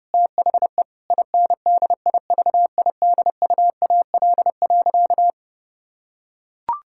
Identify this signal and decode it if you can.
Morse code, heard as a keyed tone, about 29 wpm, 700 Hz. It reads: THE INDIVIDUAL.
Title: Sentences from Top 1000 Words - Review Part 6 - 30wpm